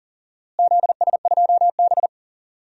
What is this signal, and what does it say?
Morse code, heard as a keyed tone, about 40 wpm, 700 Hz. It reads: ZS2B
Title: DXCC-100 Call Signs - 4 Characters - Repeated 3x - 40wpm